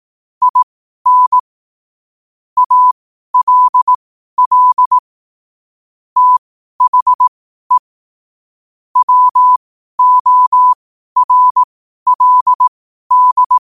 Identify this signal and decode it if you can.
Morse code, heard as a keyed tone, about 18 wpm, spaced as for 12 wpm, 1000 Hz. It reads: IN ALL THE WORLD
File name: qotdfiles / 12wpm